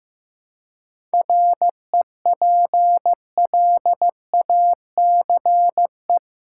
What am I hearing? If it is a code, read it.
Morse code, heard as a keyed tone, about 15 wpm, 700 Hz. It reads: REPLACE
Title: Seven Letter Words - 15wpm